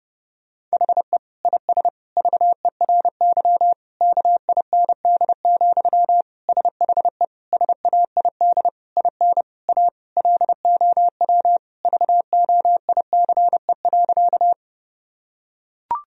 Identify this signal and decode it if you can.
Morse code, heard as a keyed tone, about 30 wpm, 700 Hz. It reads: HE IS VERY KIND, SHE SAID IN A LOW VOICE.
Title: Sentences from Top 1000 Words - Review Part 1 - 30wpm